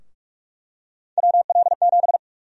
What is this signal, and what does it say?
Morse code, heard as a keyed tone, about 41 wpm, 700 Hz. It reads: WL7
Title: USA Call Sign Prefixes - Repeated 2x - 45wpm